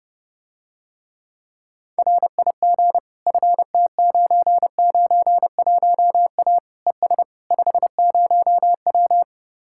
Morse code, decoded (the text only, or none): RIG FT991A ES 50W